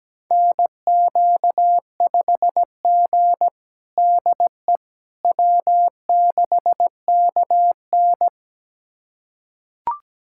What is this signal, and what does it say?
Morse code, heard as a keyed tone, about 17 wpm, 700 Hz. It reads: NQ5G DE W6KN
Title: USA Callsign DE Callsign - 4 Characters - 17wpm